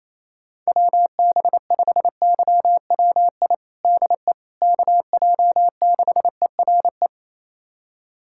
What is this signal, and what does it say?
Morse code, heard as a keyed tone, about 28 wpm, 700 Hz. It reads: WB5YWI DE KJ6ERE
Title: USA Callsign DE Callsign - 6 Characters - 28wpm